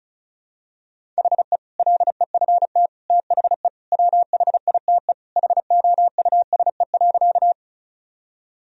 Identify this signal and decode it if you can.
Morse code, heard as a keyed tone, about 36 wpm, 700 Hz. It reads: HE LEFT THE WHITE HOUSE.
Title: Sentences from Top 1000 Words - Review Part 1 - 35wpm